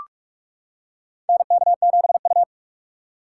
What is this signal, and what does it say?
Morse code, heard as a keyed tone, about 45 wpm, 700 Hz. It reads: NK7U
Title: K1USN SST QSO Practice - 45wpm